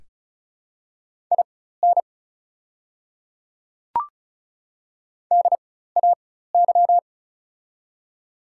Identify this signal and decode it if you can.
Morse code, heard as a keyed tone, about 36 wpm, spaced as for 16 wpm, 700 Hz. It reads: IN DAY
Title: 4x Character Spacing - Top 100 Words - 35wpm